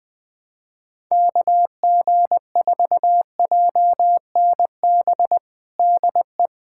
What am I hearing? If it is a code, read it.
Morse code, heard as a keyed tone, about 20 wpm, 700 Hz. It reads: KG4JNB DE
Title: USA Callsign DE Callsign 20wpm